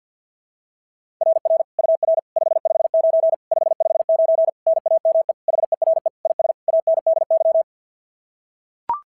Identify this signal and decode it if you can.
Morse code, heard as a keyed tone, about 46 wpm, 660 Hz. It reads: RR UR 559 559 NAME HERE IS ANDY
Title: SOTA QSO Practice 50wpm